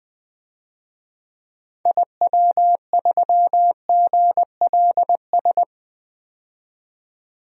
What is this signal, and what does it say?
Morse code, heard as a keyed tone, about 21 wpm, 700 Hz. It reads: IW3GLS
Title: DXCC-100 Call Signs - 6 Characters - 20wpm